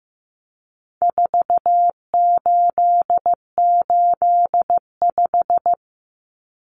Morse code, heard as a keyed tone, about 15 wpm, 700 Hz. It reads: 4885
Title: Common Cut Numbers - 4-Digit 15wpm